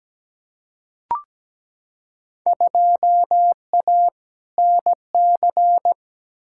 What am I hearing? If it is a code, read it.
Morse code, heard as a keyed tone, about 17 wpm, 700 Hz. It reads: 2A NC
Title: Field Day Exchanges Encore - 17wpm